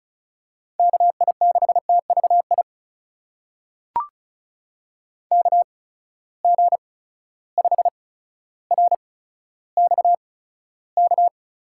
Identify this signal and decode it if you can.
Morse code, heard as a keyed tone, about 36 wpm, spaced as for 15 wpm, 700 Hz. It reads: KI6TVI KG5RXK
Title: USA Call Signs - 6 Characters - 8x Character Spacing - 35wpm